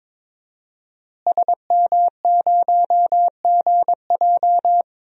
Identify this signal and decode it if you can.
Morse code, heard as a keyed tone, about 22 wpm, 700 Hz. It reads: SM0GJ
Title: DXCC-100 Call Signs - 5 Characters - Repeated 3x - 22wpm